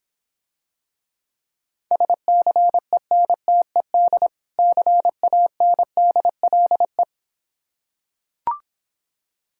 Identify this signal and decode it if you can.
Morse code, heard as a keyed tone, about 26 wpm, 700 Hz. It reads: SCENTED CANDLE